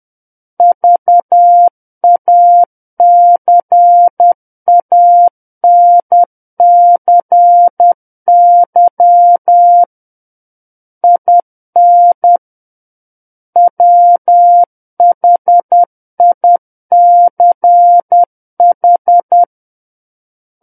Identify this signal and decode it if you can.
Morse code, heard as a keyed tone, about 10 wpm, 700 Hz. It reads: VACANCY IN WHICH
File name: Book One / 01-Chapter 1 - 10 WPM